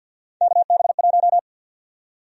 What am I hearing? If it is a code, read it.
Morse code, heard as a keyed tone, about 50 wpm, 700 Hz. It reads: KB1
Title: USA Call Sign Prefixes - Repeated 2x - 50wpm